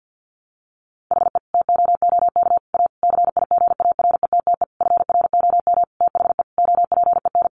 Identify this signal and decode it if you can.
Morse code, heard as a keyed tone, about 50 wpm, 700 Hz. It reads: HE TOOK A CIGARETTE FROM THE OPEN
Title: Sentences from Top 1000 Words 50wpm